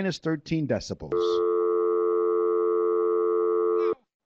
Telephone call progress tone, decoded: dial tone